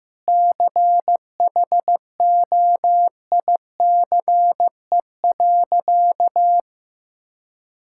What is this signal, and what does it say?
Morse code, heard as a keyed tone, about 15 wpm, 700 Hz. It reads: CHOICE.
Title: Sentences from Top 1000 Words - Review Part 6 - 15wpm